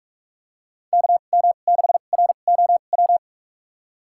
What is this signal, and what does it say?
Morse code, heard as a keyed tone, about 46 wpm, spaced as for 33 wpm, 700 Hz. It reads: KM6ROW
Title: USA Call Signs - 2x Character Spacing - 45wpm